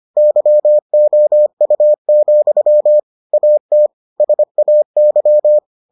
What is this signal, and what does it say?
Morse code, heard as a keyed tone, about 25 wpm, 600 Hz. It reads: YOU, AT SAY